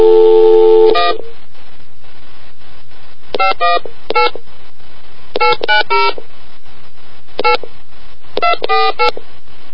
DTMF, heard as telephone touch tones, tones of about 140 ms. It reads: 1 518 890 8 248